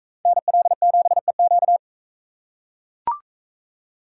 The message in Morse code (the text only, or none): NL7EQ